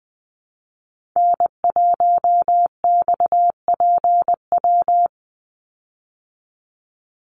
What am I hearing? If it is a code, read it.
Morse code, heard as a keyed tone, about 20 wpm, 700 Hz. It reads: N1XPW
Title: USA Call Signs 20wpm